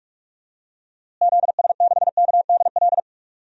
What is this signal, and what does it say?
Morse code, heard as a keyed tone, about 46 wpm, 700 Hz. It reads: ZS6KDL
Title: DXCC-100 Call Signs - 6 Characters - Repeated 2x - 45wpm